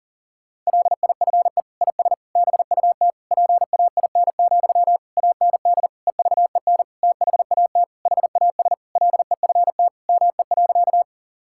Morse code, heard as a keyed tone, about 40 wpm, 700 Hz. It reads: LIFE IS BUT PAIN, AND EVEN THAT HAS LEFT ME.
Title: Sentences from Top 1600 Words 40wpm